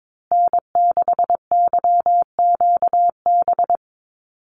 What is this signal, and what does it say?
Morse code, heard as a keyed tone, about 22 wpm, 700 Hz. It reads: N6YQB
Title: USA Call Signs Repeated 2x 22wpm